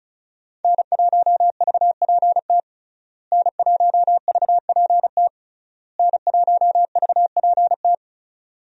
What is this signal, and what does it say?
Morse code, heard as a keyed tone, about 36 wpm, 700 Hz. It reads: N1VPT N1VPT N1VPT
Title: USA Call Signs - 5 Characters - Repeated 3x - 35wpm